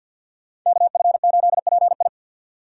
Morse code, heard as a keyed tone, about 45 wpm, 700 Hz. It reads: KF8PI
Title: USA Call Signs - 5 Characters - 6x Character Spacing - 50wpm